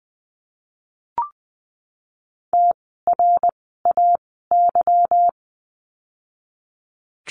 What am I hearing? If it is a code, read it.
Morse code, heard as a keyed tone, about 20 wpm, spaced as for 15 wpm, 700 Hz. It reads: TRAY
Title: 146 Lesson 24f - Letter y - Words 20wpm